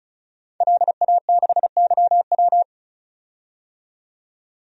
Morse code, heard as a keyed tone, about 36 wpm, 700 Hz. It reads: LA6YW
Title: DXCC-100 Call Signs - 5 Characters - 35wpm